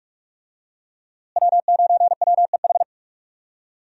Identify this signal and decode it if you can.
Morse code, heard as a keyed tone, about 46 wpm, 700 Hz. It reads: W9WEH
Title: USA Call Signs - 5 Characters - Repeated 2x - 45wpm